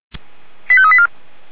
Five-tone selective call (ZVEI), heard as quick runs of five tones, tones of about 70 ms. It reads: 85384